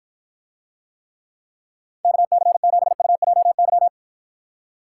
Morse code, heard as a keyed tone, about 53 wpm, 700 Hz. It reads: KC7UJY